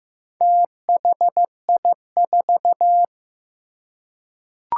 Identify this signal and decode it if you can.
Morse code, heard as a keyed tone, about 15 wpm, 700 Hz. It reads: THI4
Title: Four Challenging Letter-Number Combinations 15wpm